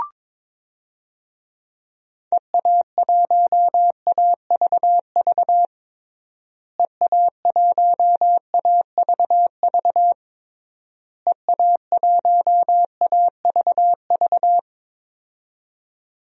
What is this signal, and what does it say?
Morse code, heard as a keyed tone, about 21 wpm, 700 Hz. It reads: EA1AVV EA1AVV EA1AVV
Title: DXCC-100 Call Signs - 6 Characters - Repeated 3x - 22wpm